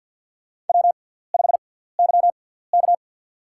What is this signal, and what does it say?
Morse code, heard as a keyed tone, about 48 wpm, spaced as for 17 wpm, 700 Hz. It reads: W5YK
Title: USA Call Signs - 6x Character Spacing - 50wpm